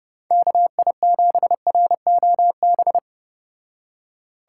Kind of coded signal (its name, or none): Morse code